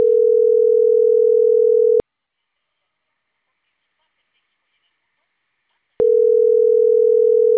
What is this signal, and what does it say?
Ringback tone (North America), 440 + 480 Hz.
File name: CENTRAL CAEA / out agcaea02